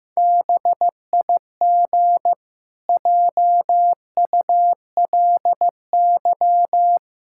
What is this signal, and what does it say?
Morse code, heard as a keyed tone, about 14 wpm, 700 Hz. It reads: BIG JULY
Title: Pangram Challenge v2 - 15wpm